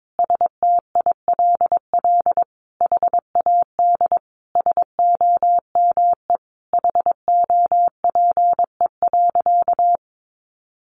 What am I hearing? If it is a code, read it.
Morse code, heard as a keyed tone, about 22 wpm, 700 Hz. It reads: STILL HAD SOME HOPE.
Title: Sentences from Top 1000 Words 22wpm